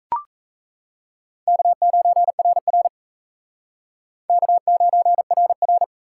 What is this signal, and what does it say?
Morse code, heard as a keyed tone, about 36 wpm, 700 Hz. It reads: K9RR K9RR